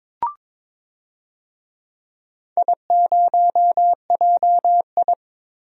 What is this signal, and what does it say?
Morse code, heard as a keyed tone, about 22 wpm, 700 Hz. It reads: I0JI
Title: DXCC-100 Call Signs - 4 Characters - Repeated 3x - 22wpm